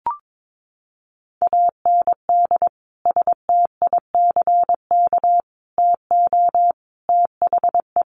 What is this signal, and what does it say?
Morse code, heard as a keyed tone, about 22 wpm, 700 Hz. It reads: AND STICK TO THE
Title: Sentences from Top 1200 Words 22wpm